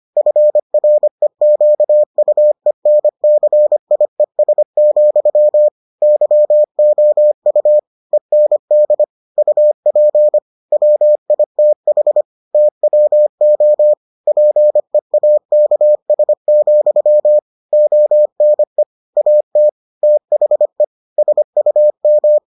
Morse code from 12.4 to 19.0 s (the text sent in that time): TWO PEAKS, ONE